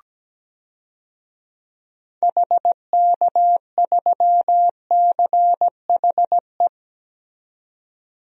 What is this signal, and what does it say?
Morse code, heard as a keyed tone, about 17 wpm, 700 Hz. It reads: HK3CHE
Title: DXCC-100 Call Signs - 6 Characters - 17wpm